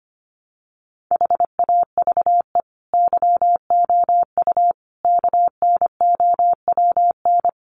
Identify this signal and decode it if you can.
Morse code, heard as a keyed tone, about 25 wpm, 700 Hz. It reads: HAVE YOU KNOWN
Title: Sentences from Top 1300 Words 25wpm